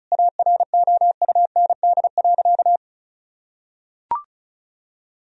Morse code, heard as a keyed tone, about 34 wpm, 700 Hz. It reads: AROUND.
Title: Sentences from Top 1000 Words - Review Part 4 - 35wpm